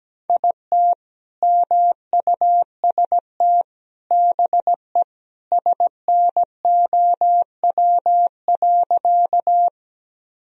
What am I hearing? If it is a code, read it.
Morse code, heard as a keyed tone, about 17 wpm, 700 Hz. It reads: IT MUST BE SNOW.
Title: Sentences from Top 1000 Words - Review Part 3 - 17wpm